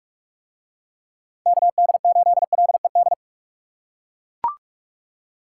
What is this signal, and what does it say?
Morse code, heard as a keyed tone, about 42 wpm, 700 Hz. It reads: KD8LED